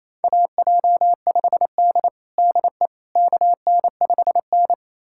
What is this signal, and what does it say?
Morse code, heard as a keyed tone, about 28 wpm, 700 Hz. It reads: AJ5D DE KN5N
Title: USA Callsign DE Callsign - 4 Characters - 28wpm